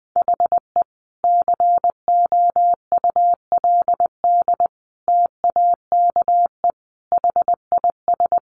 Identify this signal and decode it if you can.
Morse code, heard as a keyed tone, about 20 wpm, 700 Hz. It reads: HE COULD TAKE HIS
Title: Sentences from Top 900 Words 20wpm